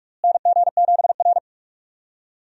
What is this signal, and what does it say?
Morse code, heard as a keyed tone, about 46 wpm, 700 Hz. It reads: NG7R
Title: USA Call Signs - 4 Characters - Repeated 2x - 45wpm